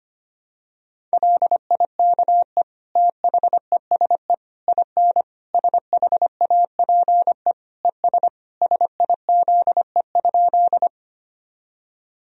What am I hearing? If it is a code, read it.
Morse code, heard as a keyed tone, about 25 wpm, 700 Hz. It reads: LIKE THESE IN SHAPE ES SIZE?
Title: Sentences from Top 500 Words and Abbreviations 25wpm